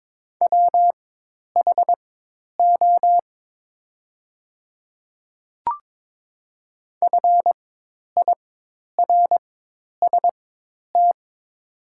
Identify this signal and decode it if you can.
Morse code, heard as a keyed tone, about 22 wpm, spaced as for 10 wpm, 700 Hz. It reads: WHO FIRST
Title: 4x Character Spacing - Top 100 Words - 22wpm